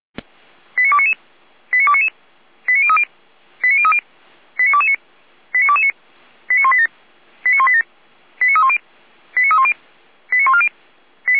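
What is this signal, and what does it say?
Five-tone selective call (ZVEI), heard as quick runs of five tones, tones of about 70 ms. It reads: 89299 89299 89030 89030 89229 89229 89187 89187 89310 89310 89140
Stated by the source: ZVEI five-tone selective call 89165 received on SDR.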